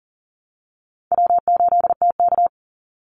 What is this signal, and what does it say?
Morse code, heard as a keyed tone, about 40 wpm, 700 Hz. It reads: W8TK
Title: Easy Call Signs Repeated 2x - 40wpm